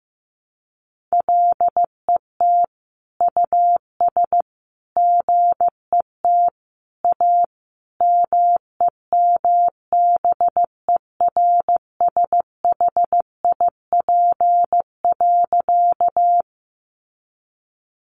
Morse code, heard as a keyed tone, about 15 wpm, 700 Hz. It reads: LET US GET A MEMBERSHIP.
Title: Sentences from Top 1400 Words 15wpm